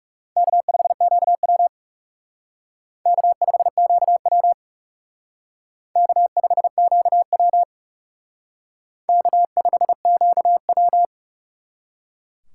Morse code, heard as a keyed tone, about 36 wpm, 700 Hz. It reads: K5QW K5QW K5QW K5QW
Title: Speed Racing - USA 1x2 Call Signs 30wpm